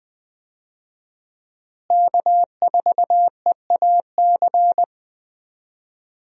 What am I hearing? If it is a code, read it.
Morse code, heard as a keyed tone, about 20 wpm, 700 Hz. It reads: K4EAC
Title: USA Call Signs - 5 Characters - Repeated 3x - 20wpm